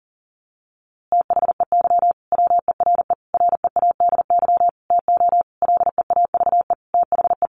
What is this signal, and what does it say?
Morse code, heard as a keyed tone, about 37 wpm, 700 Hz. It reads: THEY WERE READY TO LEAVE THE
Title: A Sound Of Thunder 40wpm